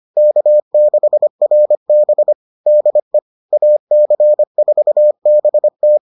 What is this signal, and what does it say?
Morse code, heard as a keyed tone, about 25 wpm, 600 Hz. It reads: K6RB DE AC4BT